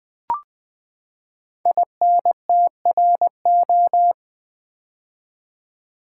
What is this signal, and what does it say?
Morse code, heard as a keyed tone, about 20 wpm, 700 Hz. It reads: INTRO
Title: Build A Word - 5 Letters At A Time - 12-Letter Words - 20wpm